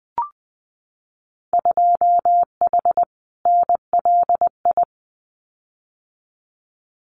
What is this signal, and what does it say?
Morse code, heard as a keyed tone, about 20 wpm, 700 Hz. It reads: 2H NLI
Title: Winter Field Day Exchanges Encore 20wpm